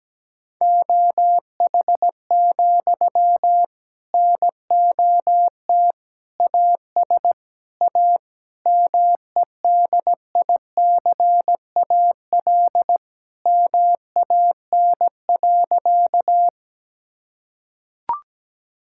Morse code, heard as a keyed tone, about 17 wpm, 700 Hz. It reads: OH, NOT AS A MEDICAL MAN.
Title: Sentences from Top 1000 Words - Review Part 5 - 17wpm